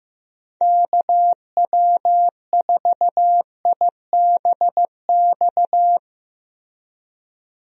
Morse code, heard as a keyed tone, about 15 wpm, 700 Hz. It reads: KW4IBX
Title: USA Call Signs - 6 Characters - Repeated 3x - 15wpm